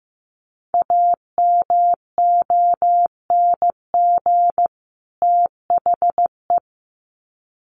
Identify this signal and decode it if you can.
Morse code, heard as a keyed tone, about 15 wpm, 700 Hz. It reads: AMONG THE
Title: Sets of 2 Words 15wpm